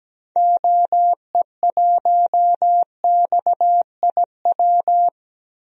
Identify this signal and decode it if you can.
Morse code, heard as a keyed tone, about 17 wpm, 700 Hz. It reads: OE1XIW
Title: DXCC-100 Call Signs - 6 Characters - 17wpm